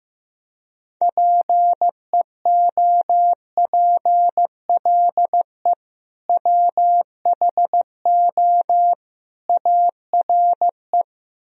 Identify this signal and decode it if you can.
Morse code, heard as a keyed tone, about 15 wpm, 700 Hz. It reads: PEOPLE WHO ARE
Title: Speed-Racing - Sets of 3 Words 15wpm